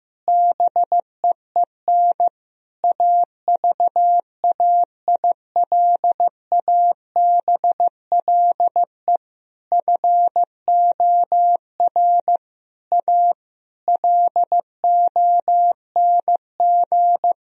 Morse code, heard as a keyed tone, about 15 wpm, 700 Hz. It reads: BEEN AVAILABLE FOR A LONG
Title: Sentences from Top 1000 Words - Review Part 4 - 15wpm